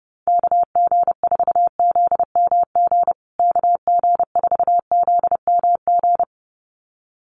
Morse code, heard as a keyed tone, about 30 wpm, 700 Hz. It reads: KG4ZMG KG4ZMG
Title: USA Call Signs Repeated 2x 30wpm